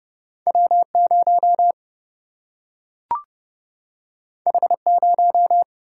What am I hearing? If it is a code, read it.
Morse code, heard as a keyed tone, about 30 wpm, 700 Hz. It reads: W0 H0
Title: Single-Letter Single-Number 30wpm